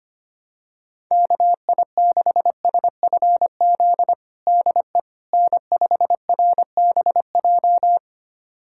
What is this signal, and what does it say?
Morse code, heard as a keyed tone, about 25 wpm, 700 Hz. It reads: KI6SFZ DE N5RBJ
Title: USA Callsign DE Callsign 25wpm